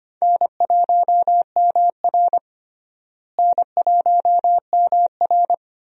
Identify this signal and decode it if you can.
Morse code, heard as a keyed tone, about 25 wpm, 700 Hz. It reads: N1MR N1MR
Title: USA Call Signs - 4 Characters - Repeated 2x - 25wpm